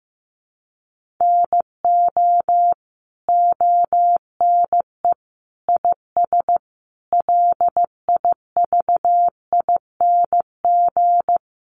Morse code, heard as a keyed tone, about 15 wpm, 700 Hz. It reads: NO ONE IS LIVING
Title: Sentences from Top 1100 Words 15wpm